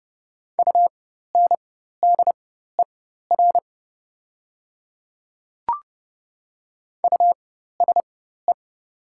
Morse code, heard as a keyed tone, about 30 wpm, spaced as for 14 wpm, 700 Hz. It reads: UNDER USE
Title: 4x Character Spacing - Top 300 Words - 30wpm